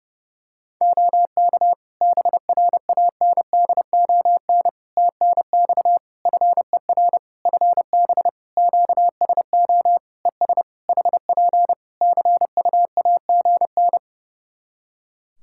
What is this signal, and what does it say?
Morse code, heard as a keyed tone, about 28 wpm, 700 Hz. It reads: OK BRANDON TNX FER FB QSO ES HP CUAGN
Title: K1USN SST QSO Practice 30wpm